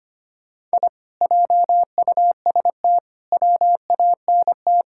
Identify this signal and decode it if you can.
Morse code, heard as a keyed tone, about 25 wpm, 700 Hz. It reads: I JUST WANT
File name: practice / 25wpm / Sentences with 2-Word Phrases - 25wpm